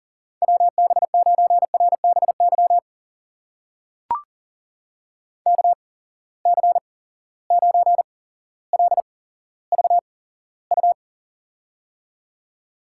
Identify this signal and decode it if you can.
Morse code, heard as a keyed tone, about 38 wpm, spaced as for 17 wpm, 700 Hz. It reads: WB9RBY KC9LVU
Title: USA Call Signs - 6 Characters - 8x Character Spacing - 40wpm